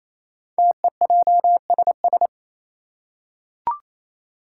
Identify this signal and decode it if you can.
Morse code, heard as a keyed tone, about 27 wpm, 700 Hz. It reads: TEJSS
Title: Code Groups - 5 Letters - 28wpm